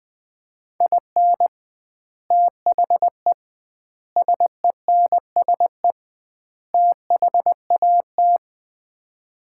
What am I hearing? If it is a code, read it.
Morse code, heard as a keyed tone, about 20 wpm, 700 Hz. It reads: IN THE SENSE THAT
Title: 2x Word-Spacing - Sets of 4 Words - 20wpm